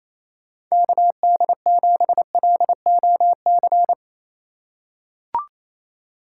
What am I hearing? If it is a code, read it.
Morse code, heard as a keyed tone, about 27 wpm, 700 Hz. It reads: KD7LOC
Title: USA Call Signs - 6 Characters - 4x Character Spacing - 28wpm